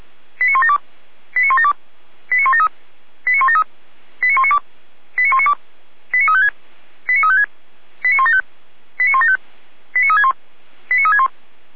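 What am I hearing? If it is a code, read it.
Five-tone selective call (ZVEI), heard as quick runs of five tones, tones of about 70 ms. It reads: 89172 89172 89173 89173 89192 89192 89356 89356 89175 89175 89361 89361
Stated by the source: ZVEI five-tone call received by SDR, recorded by Feuerwehr Orlamünde (f85360).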